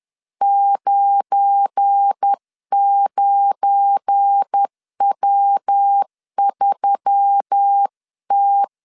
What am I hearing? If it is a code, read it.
Morse code, heard as a keyed tone, about 10 wpm, 800 Hz. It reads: 99W3T
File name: LEZIONI / 7 GF9Z / 7 GF9 10wpm 2747